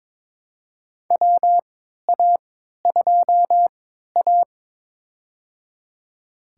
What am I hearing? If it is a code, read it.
Morse code, heard as a keyed tone, about 22 wpm, spaced as for 12 wpm, 700 Hz. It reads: WA2A
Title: USA Call Signs - 3x Character Spacing - 22wpm